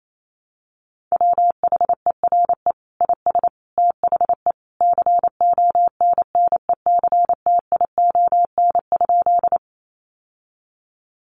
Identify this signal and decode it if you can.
Morse code, heard as a keyed tone, about 28 wpm, 700 Hz. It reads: WHERE IS THE CONNECTION?
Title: Sentences from Top 1300 Words 28wpm